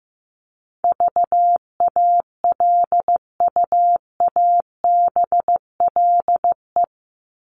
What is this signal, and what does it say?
Morse code, heard as a keyed tone, about 15 wpm, 700 Hz. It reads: VALUABLE
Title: Top 1001-1250 Words 15wpm